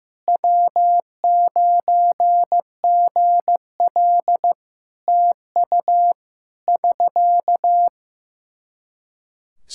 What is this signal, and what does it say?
Morse code, heard as a keyed tone, about 15 wpm, 700 Hz. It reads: W9GL TU <SK>
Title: Practice QSO - Exchange 4 - Ending - 15wpm